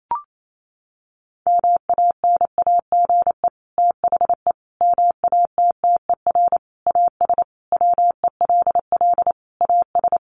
Morse code, heard as a keyed tone, about 28 wpm, 700 Hz. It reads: MANAGE THE MATTER AS WELL AS
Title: Sentences from Top 1400 Words 28wpm